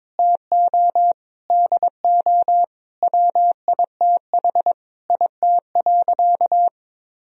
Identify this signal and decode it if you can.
Morse code, heard as a keyed tone, about 22 wpm, 700 Hz. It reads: TO DO WITH IT.
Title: Sentences from Top 1000 Words - Review Part 2 - 22wpm